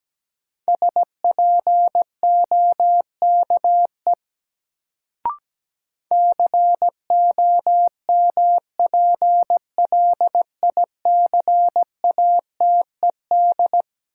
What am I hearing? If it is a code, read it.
Morse code, heard as a keyed tone, about 17 wpm, 700 Hz. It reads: SPOKE COMPLICATED
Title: Sentences from Top 1200 Words - New Word Drill - 17wpm